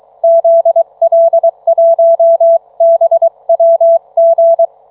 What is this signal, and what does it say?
Morse code, heard as a keyed tone, about 22 wpm, 720 Hz. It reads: ZL1BWG